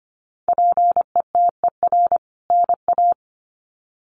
Morse code, heard as a keyed tone, about 25 wpm, 700 Hz. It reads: PETER NA
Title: CWT Exchange 25wpm